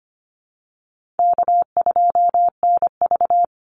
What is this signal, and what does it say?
Morse code, heard as a keyed tone, about 25 wpm, 700 Hz. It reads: K2NV
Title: Easy Call Signs Repeated 3x - 25wpm